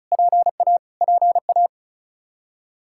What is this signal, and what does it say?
Morse code, heard as a keyed tone, about 34 wpm, 700 Hz. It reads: PA PA
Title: Practice QSO - Exchange 1 - Beginning - 35wpm